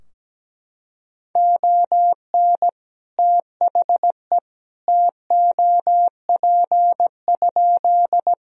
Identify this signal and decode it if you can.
Morse code, heard as a keyed tone, about 17 wpm, 700 Hz. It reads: ON THE TOP?
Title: Sentences from Top 1000 Words - Review Part 2 - 17wpm